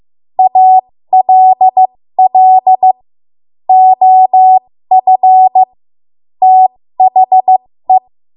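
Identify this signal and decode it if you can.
Morse code, heard as a keyed tone, about 15 wpm, spaced as for 13 wpm, 750 Hz. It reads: ALL OF THE
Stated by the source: 13 wpm Morse code practice archive transmission.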